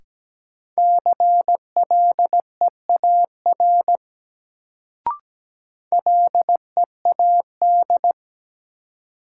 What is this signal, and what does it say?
Morse code, heard as a keyed tone, about 17 wpm, 700 Hz. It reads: CLEAR LEAD
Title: Sentences from Top 500 Words - New Word Drill - 17wpm